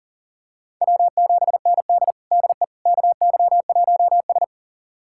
Morse code, heard as a keyed tone, about 40 wpm, 700 Hz. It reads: W7ND DE KY1S
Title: USA Callsign DE Callsign - 4 Characters - 40wpm